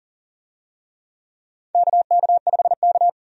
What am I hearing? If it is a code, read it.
Morse code, heard as a keyed tone, about 40 wpm, 700 Hz. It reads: KK5K
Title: USA Call Signs - 4 Characters - Repeated 2x - 40wpm